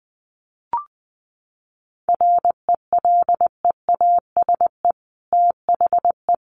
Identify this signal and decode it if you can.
Morse code, heard as a keyed tone, about 20 wpm, 700 Hz. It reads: RELEASE THE
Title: Sentences from Top 1100 Words 20wpm